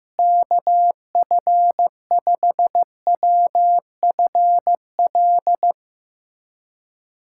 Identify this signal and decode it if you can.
Morse code, heard as a keyed tone, about 15 wpm, 700 Hz. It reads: KF5WFL
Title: USA Callsign DE Callsign 15wpm